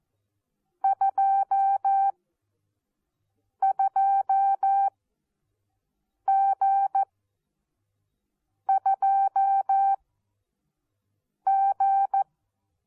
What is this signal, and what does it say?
Morse code, heard as a keyed tone, about 14 wpm, spaced as for 5 wpm, 780 Hz. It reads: 22G2G
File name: Onlinelehrgang / morsen / DJ4UF Morsen14